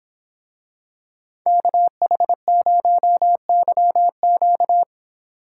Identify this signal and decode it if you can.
Morse code, heard as a keyed tone, about 26 wpm, 700 Hz. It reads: KH0YQ